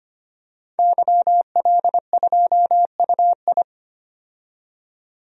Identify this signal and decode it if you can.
Morse code, heard as a keyed tone, about 25 wpm, 700 Hz. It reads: YL2UI
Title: DXCC-100 Call Signs - 5 Characters - 25wpm